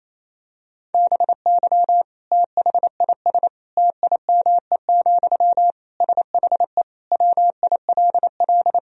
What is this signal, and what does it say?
Morse code, heard as a keyed tone, about 28 wpm, 700 Hz. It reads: BY THIS TIME, SHE WILL
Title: Sentences from Top 1000 Words - Review Part 4 - 28wpm